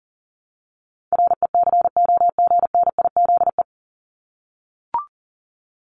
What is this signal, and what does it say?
Morse code, heard as a keyed tone, about 38 wpm, 700 Hz. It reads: RECOGNIZE
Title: Top 1251-1500 Words 40wpm